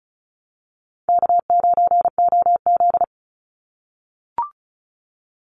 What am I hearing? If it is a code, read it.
Morse code, heard as a keyed tone, about 36 wpm, 700 Hz. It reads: K9OZ
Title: NA SOTA Chaser Callsigns 35wpm